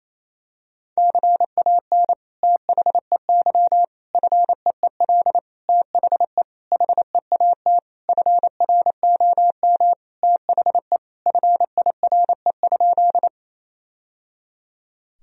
Morse code, heard as a keyed tone, about 28 wpm, 700 Hz. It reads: CAN THEY FEEL THE HEAT FROM THE FIRE?
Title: Sentences from Top 1000 Words - Review Part 3 - 28wpm